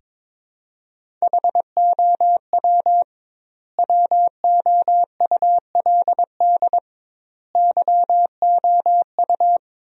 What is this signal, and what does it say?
Morse code, heard as a keyed tone, about 22 wpm, 700 Hz. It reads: HOW WOULD YOU
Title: 2x Word-Spacing - Sentences from Top 100 Words - 22wpm